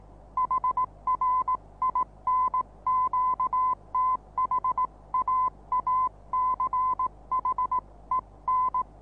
Morse code, heard as a keyed tone, about 18 wpm, 1020 Hz. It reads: HR IN QTH AACHEN
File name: repos / Morsekurs / DJ4UF Morsen26